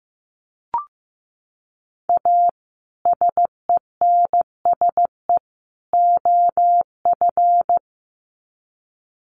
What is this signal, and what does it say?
Morse code, heard as a keyed tone, about 15 wpm, 700 Hz. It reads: A SENSE OF
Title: Sets of 3 Words 15wpm